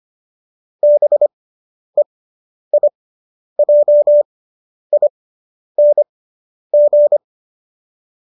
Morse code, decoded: BEIJING

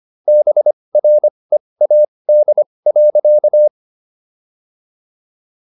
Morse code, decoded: BREAD.

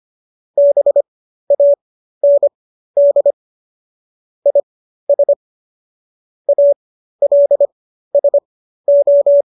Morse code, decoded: BAND IS ALSO

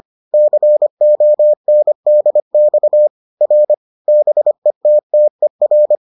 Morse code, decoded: CONDX R BETTER

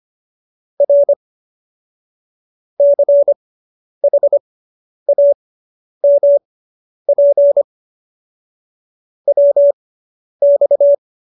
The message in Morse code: R CHAMP WX

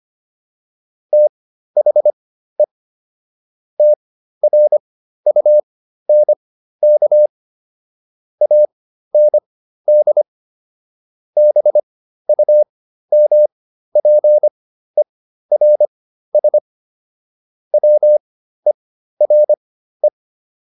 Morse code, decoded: THE TRUNK AND BUMPERS WERE

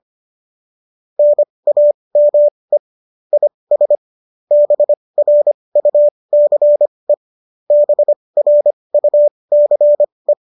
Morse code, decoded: NAME IS BRUCE BRUCE

